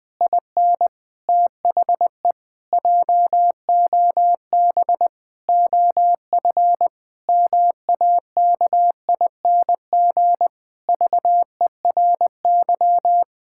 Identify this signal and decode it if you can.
Morse code, heard as a keyed tone, about 21 wpm, 700 Hz. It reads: IN THE JOB OF MAKING VERY
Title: Pangram Challenge v2 - 20wpm